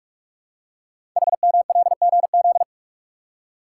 Morse code, heard as a keyed tone, about 44 wpm, 700 Hz. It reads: HMLGZ
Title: Code Groups - 5 Letters - 45wpm